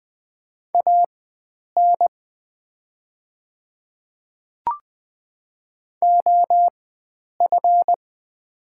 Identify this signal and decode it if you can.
Morse code, heard as a keyed tone, about 20 wpm, spaced as for 9 wpm, 700 Hz. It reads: AN OF
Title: 4x Character Spacing - Top 100 Words - 20wpm